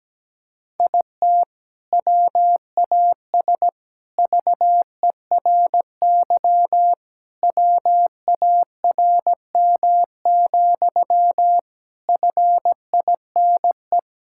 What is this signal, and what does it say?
Morse code, heard as a keyed tone, about 17 wpm, 700 Hz. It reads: IT WAS VERY WARM, FINE
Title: Sentences from Top 1000 Words - Review Part 5 - 17wpm